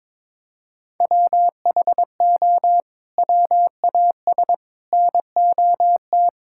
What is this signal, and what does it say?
Morse code, heard as a keyed tone, about 23 wpm, 700 Hz. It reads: WHO WAS NOT
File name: practice / 22wpm / Sets of 7 Words - 22wpm